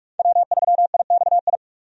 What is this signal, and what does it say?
Morse code, heard as a keyed tone, about 43 wpm, 700 Hz. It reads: W3IXI
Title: USA Call Signs - 5 Characters - 45wpm